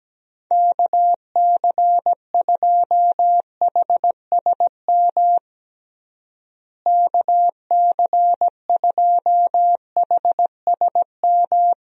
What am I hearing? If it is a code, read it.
Morse code, heard as a keyed tone, about 17 wpm, 700 Hz. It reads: KC2HSM KC2HSM
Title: USA Call Signs - 6 Characters - Repeated 2x - 17wpm